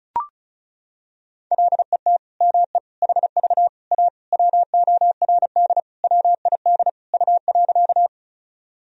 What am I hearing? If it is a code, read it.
Morse code, heard as a keyed tone, about 36 wpm, 700 Hz. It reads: LET ME HV A WORD WID U.
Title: Sentences from Top 300 Words and Abbreviations 35wpm